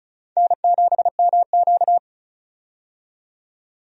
Morse code, heard as a keyed tone, about 34 wpm, 700 Hz. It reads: N7MQ
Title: USA Call Signs - 4 Characters - 35wpm